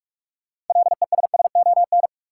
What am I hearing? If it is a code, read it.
Morse code, heard as a keyed tone, about 45 wpm, 700 Hz. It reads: LESSON
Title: Sentences from Top 1200 Words - New Word Drill - 45wpm